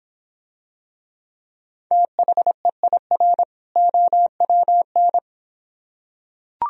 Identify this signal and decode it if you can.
Morse code, heard as a keyed tone, about 26 wpm, 700 Hz. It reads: THEIR OWN